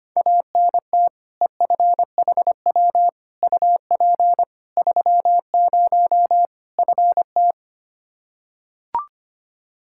Morse code, heard as a keyed tone, about 25 wpm, 700 Hz. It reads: ANT EFHW UP 30 FT
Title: Practice QSO - Exchange 2 - Info - 25wpm